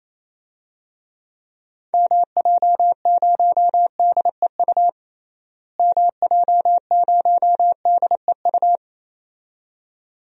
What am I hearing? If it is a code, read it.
Morse code, heard as a keyed tone, about 28 wpm, 700 Hz. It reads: MJ0DEU MJ0DEU
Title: DXCC-100 Call Signs - 6 Characters - Repeated 2x - 28wpm